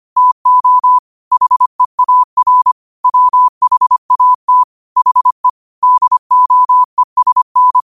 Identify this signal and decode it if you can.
Morse code, heard as a keyed tone, about 25 wpm, 1000 Hz. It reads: TO HEAR WHAT HE DOESN